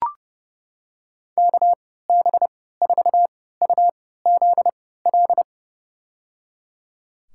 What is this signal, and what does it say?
Morse code, heard as a keyed tone, about 30 wpm, spaced as for 17 wpm, 700 Hz. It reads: KB4UZL
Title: USA Call Signs - 3x Character Spacing - 30wpm